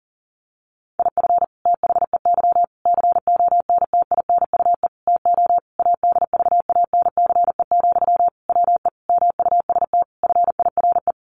Morse code, heard as a keyed tone, about 40 wpm, 700 Hz. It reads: IF THEY CONTINUE TO ADVANCE, WE MUST FIRE
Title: Sentences from Top 1400 Words 40wpm